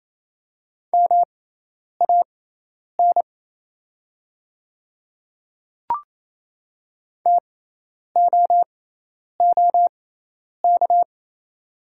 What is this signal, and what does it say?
Morse code, heard as a keyed tone, about 28 wpm, spaced as for 10 wpm, 700 Hz. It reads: MAN TOOK
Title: 6x Character Spacing - Top 500 Words - 28wpm